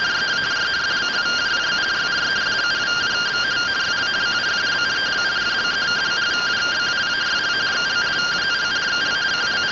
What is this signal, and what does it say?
Two-tone teleprinter signal, about 170 Hz shift.